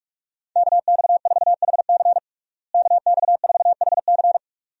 Morse code, heard as a keyed tone, about 41 wpm, 700 Hz. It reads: KX4HC KX4HC
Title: USA Call Signs - 5 Characters - Repeated 2x - 45wpm